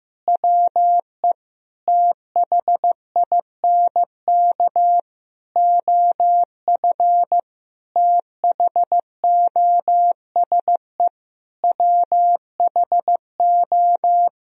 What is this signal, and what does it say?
Morse code, heard as a keyed tone, about 15 wpm, 700 Hz. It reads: WE THINK OF THOSE WHO
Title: Sentences from Top 1000 Words - Review Part 2 - 15wpm